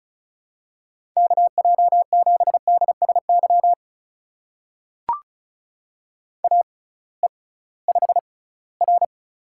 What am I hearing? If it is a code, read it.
Morse code, heard as a keyed tone, about 36 wpm, spaced as for 21 wpm, 700 Hz. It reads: KJ7DSY AE5R
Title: USA Call Signs - 6x Character Spacing - 35wpm